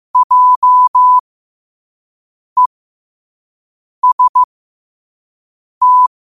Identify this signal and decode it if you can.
Morse code, heard as a keyed tone, about 15 wpm, spaced as for 5 wpm, 1000 Hz. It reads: JEST
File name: qotdfiles / 5wpm